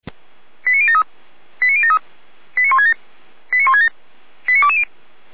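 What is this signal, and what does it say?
Five-tone selective call (ZVEI), heard as quick runs of five tones, tones of about 70 ms. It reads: 89073 89073 89167 89167 89220
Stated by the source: SDR receiver recording of ZVEI five-tone selective call 89022, from feuerwehr-orlamuende.org fire brigade server.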